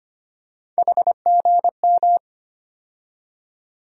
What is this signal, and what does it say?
Morse code, heard as a keyed tone, about 25 wpm, 700 Hz. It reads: HGM
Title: USA Call Sign Postfixes - 3 Letters 25wpm